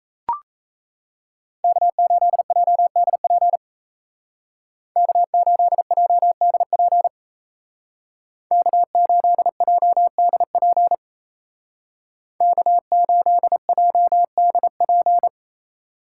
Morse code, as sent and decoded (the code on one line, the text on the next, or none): -.- ---.. .--- -.. .--. / -.- ---.. .--- -.. .--. / -.- ---.. .--- -.. .--. / -.- ---.. .--- -.. .--.
K8JDP K8JDP K8JDP K8JDP